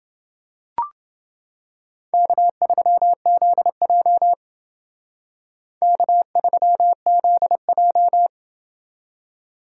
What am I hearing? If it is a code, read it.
Morse code, heard as a keyed tone, about 30 wpm, 700 Hz. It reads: K3ZJ K3ZJ